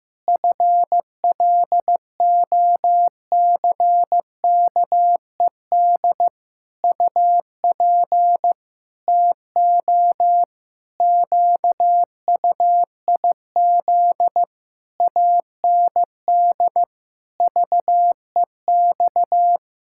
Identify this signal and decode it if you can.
Morse code, heard as a keyed tone, about 15 wpm, 700 Hz. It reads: FLOCKED UP TO QUIZ AND VEX
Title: Pangram Challenge v2 - 15wpm